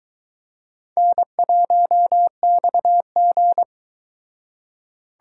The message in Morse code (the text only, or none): N1XG